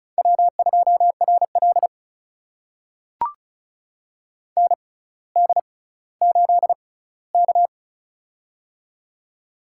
Morse code, heard as a keyed tone, about 36 wpm, spaced as for 18 wpm, 700 Hz. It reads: W2RL ND8K
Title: USA Call Signs - 4 Characters - 6x Character Spacing - 35wpm